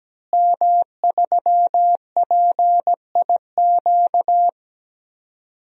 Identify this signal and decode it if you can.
Morse code, heard as a keyed tone, about 17 wpm, 700 Hz. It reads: M3PIQ
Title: DXCC-100 Call Signs - 5 Characters - Repeated 2x - 17wpm